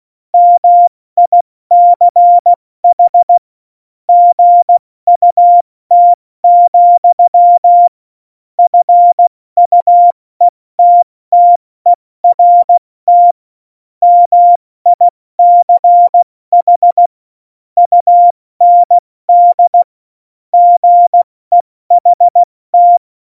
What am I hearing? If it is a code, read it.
Morse code, heard as a keyed tone, about 16 wpm, 700 Hz. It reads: MICH GUT, FUETTERT MICH UND GEHT
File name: Morsekurs / Witze / Witz 06. Oktober 2020, 70 F80